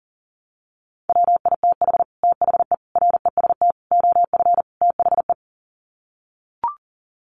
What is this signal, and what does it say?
Morse code, heard as a keyed tone, about 37 wpm, 700 Hz. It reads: WITH THE REST OF THE
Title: Sets of 5 Words 40wpm